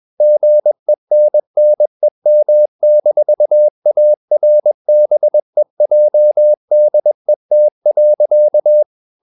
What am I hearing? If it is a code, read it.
Morse code, heard as a keyed tone, about 21 wpm, 600 Hz. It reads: GENNEM-ARBEJDET.
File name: Ferie QRS 21wpm